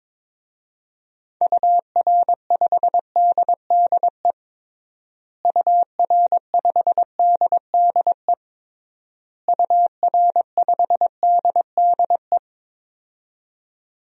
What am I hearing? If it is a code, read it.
Morse code, heard as a keyed tone, about 21 wpm, 700 Hz. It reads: UR5DDE UR5DDE UR5DDE
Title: DXCC-100 Call Signs - 6 Characters - Repeated 3x - 22wpm